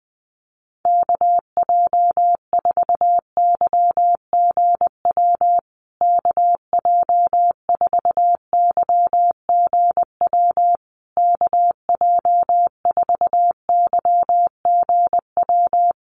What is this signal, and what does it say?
Morse code, heard as a keyed tone, about 19 wpm, 700 Hz. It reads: KJ4YGW KJ4YGW KJ4YGW
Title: USA Call Signs Repeated 3x 20wpm